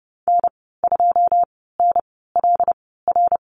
Morse code, heard as a keyed tone, about 30 wpm, spaced as for 17 wpm, 700 Hz. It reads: N2NLR
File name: Lesson 15g - Number 5 - Call Signs 30wpm